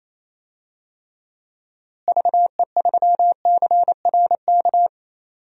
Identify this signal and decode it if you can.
Morse code, heard as a keyed tone, about 27 wpm, 700 Hz. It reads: VE3CRK
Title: DXCC-100 Call Signs - 6 Characters - Repeated 3x - 28wpm